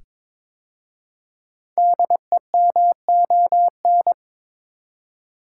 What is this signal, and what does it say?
Morse code, heard as a keyed tone, about 22 wpm, 700 Hz. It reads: DEMON
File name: 5 Letter Words - First 3 Letters Given - 22wpm